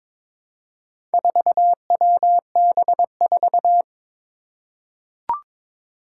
Morse code, heard as a keyed tone, about 22 wpm, 700 Hz. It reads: 4WB4
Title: Four Challenging Letter-Number Combinations 22wpm